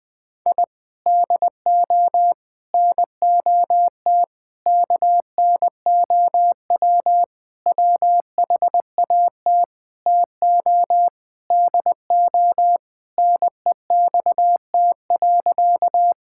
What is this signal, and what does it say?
Morse code, heard as a keyed tone, about 20 wpm, 700 Hz. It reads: I DO NOT KNOW WHAT TO DO NEXT.
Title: Sentences with 5-Word Phrases - 20wpm